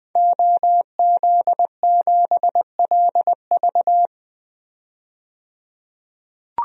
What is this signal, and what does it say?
Morse code, heard as a keyed tone, about 21 wpm, 700 Hz. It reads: OZ7LV
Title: DXCC-100 Call Signs - 5 Characters - 20wpm